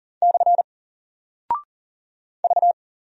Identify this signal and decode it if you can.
Morse code, heard as a keyed tone, about 41 wpm, 700 Hz. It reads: / V
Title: Single Character (Letter:Number:Punctuation:Prosign) 40wpm